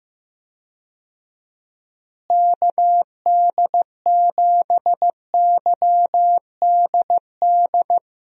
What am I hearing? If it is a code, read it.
Morse code, heard as a keyed tone, about 15 wpm, 700 Hz. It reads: KD7YDD
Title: USA Call Signs - 6 Characters - Repeated 2x - 15wpm